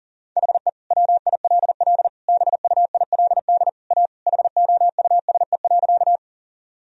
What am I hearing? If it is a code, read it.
Morse code, heard as a keyed tone, about 40 wpm, 700 Hz. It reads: HE WILL BUILD A HOUSE.
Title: Sentences from Top 1000 Words - Review Part 1 - 40wpm